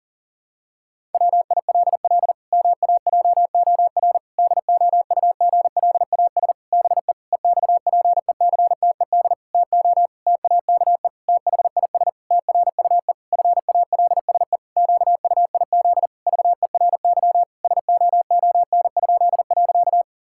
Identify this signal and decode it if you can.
Morse code, heard as a keyed tone, about 40 wpm, 700 Hz. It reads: WILL MAJOR DOUGLAS BE EXPECTED TO TAKE THIS TRUE FALSE QUIZ VERY SOON?.
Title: Pangram Challenge v2 - 40wpm